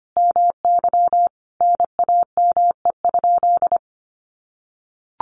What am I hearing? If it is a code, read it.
Morse code, heard as a keyed tone, about 25 wpm, 700 Hz. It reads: MY NAME?
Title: Sentences from Top 1400 Words 25wpm